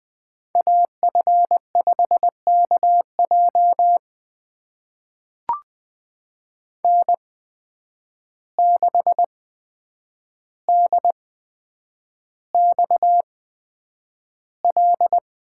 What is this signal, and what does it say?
Morse code, heard as a keyed tone, about 21 wpm, spaced as for 9 wpm, 700 Hz. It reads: AF5KJ N6DXL
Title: USA Call Signs - 5 Characters - 8x Character Spacing - 20wpm